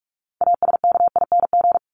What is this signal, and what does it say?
Morse code, heard as a keyed tone, about 44 wpm, 700 Hz. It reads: ASKING
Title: ING Suffix - Top 100 Words with Base Words 45wpm